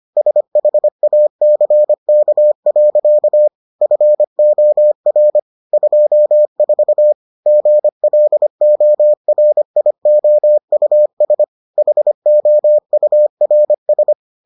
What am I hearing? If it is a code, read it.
Morse code, heard as a keyed tone, about 25 wpm, 600 Hz. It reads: SHACK. FOR 24 GLORIOUS HOURS